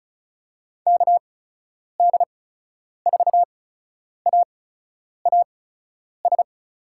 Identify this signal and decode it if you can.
Morse code, heard as a keyed tone, about 36 wpm, spaced as for 10 wpm, 700 Hz. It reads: KD4AAS
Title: USA Call Signs - 6 Characters - 8x Character Spacing - 35wpm